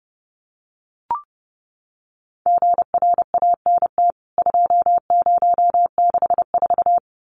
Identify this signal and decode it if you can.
Morse code, heard as a keyed tone, about 28 wpm, 700 Hz. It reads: GRANT 2064
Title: CWT Exchange 30wpm